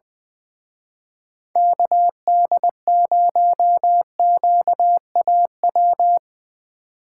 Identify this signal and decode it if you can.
Morse code, heard as a keyed tone, about 23 wpm, 700 Hz. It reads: KD0QAW
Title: Speed Racing - USA Call Signs 20wpm